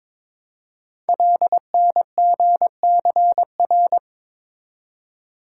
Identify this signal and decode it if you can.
Morse code, heard as a keyed tone, about 22 wpm, 700 Hz. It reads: LNGCR
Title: Code Groups - 5 Letters - 22wpm